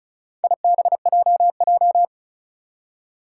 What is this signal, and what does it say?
Morse code, heard as a keyed tone, about 36 wpm, 700 Hz. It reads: IBJJ
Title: Four Challenging Letter-Number Combinations 35wpm